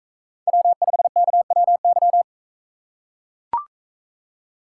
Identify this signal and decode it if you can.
Morse code, heard as a keyed tone, about 38 wpm, 700 Hz. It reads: W5KWY